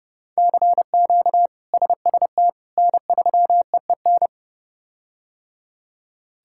Morse code, heard as a keyed tone, about 30 wpm, 700 Hz. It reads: CQ SST N3EEN
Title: K1USN SST QSO Practice - 30wpm